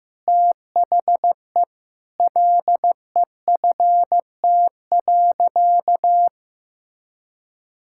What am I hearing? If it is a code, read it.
Morse code, heard as a keyed tone, about 16 wpm, 700 Hz. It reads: THE LEFT.
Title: Sentences from Top 1000 Words - Review Part 5 - 15wpm